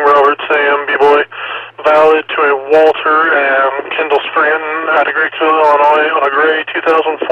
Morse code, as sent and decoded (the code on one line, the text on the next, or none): none